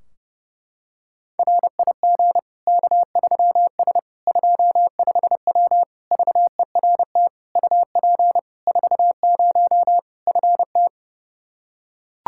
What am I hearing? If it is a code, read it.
Morse code, heard as a keyed tone, about 32 wpm, 700 Hz. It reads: RIG K3S 25W VERT UP 40 FT
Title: K1USN SST QSO Practice 30wpm